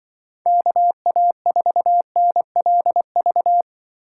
Morse code, heard as a keyed tone, about 24 wpm, 700 Hz. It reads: KA4NLV
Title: Speed Racing - USA 2x3 Call Signs 20wpm